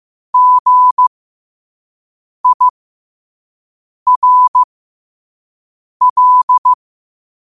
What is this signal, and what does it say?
Morse code, heard as a keyed tone, about 15 wpm, spaced as for 5 wpm, 1000 Hz. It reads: GIRL
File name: qotdfiles / 5wpm